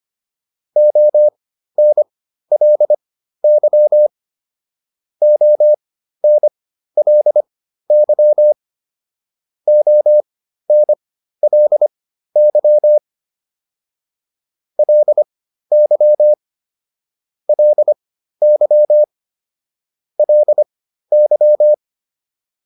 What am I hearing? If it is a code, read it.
Morse code, heard as a keyed tone, about 25 wpm, spaced as for 13 wpm, 600 Hz. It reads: ONLY ONLY ONLY LY LY LY